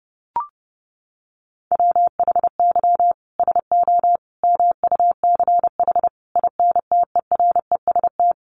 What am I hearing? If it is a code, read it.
Morse code, heard as a keyed tone, about 30 wpm, 700 Hz. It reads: WHY SO MUCH INTEREST
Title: Sentences from Top 1200 Words 30wpm